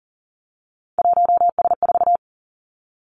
Morse code, heard as a keyed tone, about 41 wpm, 700 Hz. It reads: 1S4
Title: Three Challenging Letter-Number Combinations 40wpm